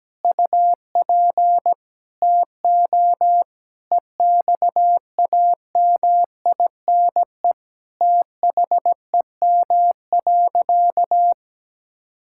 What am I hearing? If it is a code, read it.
Morse code, heard as a keyed tone, about 17 wpm, 700 Hz. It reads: UP TO EXAMINE THEM.
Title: Sentences from Top 1600 Words 17wpm